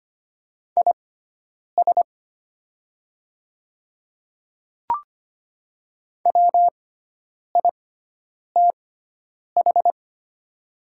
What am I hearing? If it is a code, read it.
Morse code, heard as a keyed tone, about 25 wpm, spaced as for 9 wpm, 700 Hz. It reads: IS WITH
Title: 6x Character Spacing - Top 100 Words - 25wpm